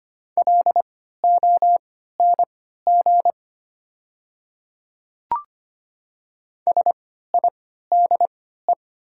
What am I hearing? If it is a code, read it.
Morse code, heard as a keyed tone, about 25 wpm, spaced as for 14 wpm, 700 Hz. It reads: LONG SIDE
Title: 3x Character Spacing - Top 100 Words - 25wpm